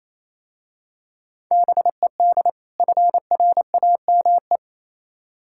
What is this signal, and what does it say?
Morse code, heard as a keyed tone, about 27 wpm, 690 Hz. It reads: BED FRAME
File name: Easy Words - Speed-Racing with Hints - 22wpm